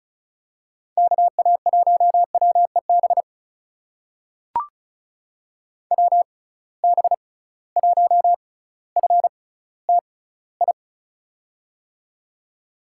Morse code, heard as a keyed tone, about 36 wpm, spaced as for 18 wpm, 700 Hz. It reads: KA1WEB WB1FTI
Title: USA Call Signs - 6 Characters - 6x Character Spacing - 35wpm